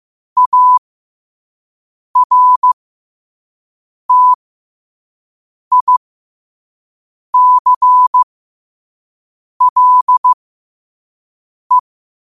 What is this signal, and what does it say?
Morse code, heard as a keyed tone, about 15 wpm, spaced as for 5 wpm, 1000 Hz. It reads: ARTICLE
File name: qotdfiles / 5wpm